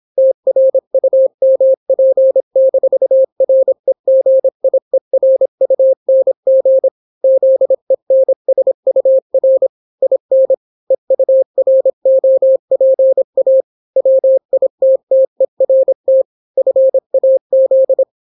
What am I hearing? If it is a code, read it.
Morse code, heard as a keyed tone, about 26 wpm, 530 Hz. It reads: TRUMP-REGIERUNG ZENSUR IN EUROPA WITTERT FAZ